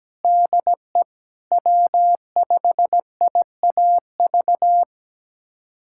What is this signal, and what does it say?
Morse code, heard as a keyed tone, about 17 wpm, 700 Hz. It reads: DE W5IAV
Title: USA Callsign DE Callsign 17wpm